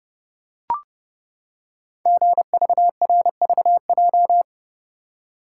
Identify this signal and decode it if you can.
Morse code, heard as a keyed tone, about 30 wpm, 700 Hz. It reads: GVRVJ
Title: Code Groups - 5 Letter:Numbers - 30wpm